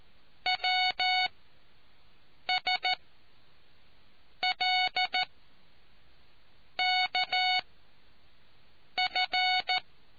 Morse code, heard as a keyed tone, about 14 wpm, spaced as for 5 wpm, 740 Hz. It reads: WSLKF